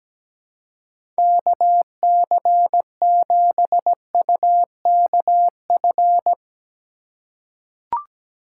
Morse code, heard as a keyed tone, about 17 wpm, 700 Hz. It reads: KC7UKF
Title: USA Call Signs - 2x Character Spacing - 17wpm